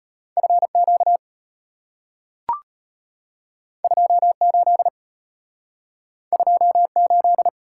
Morse code, heard as a keyed tone, about 34 wpm, 690 Hz. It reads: FQ 28 28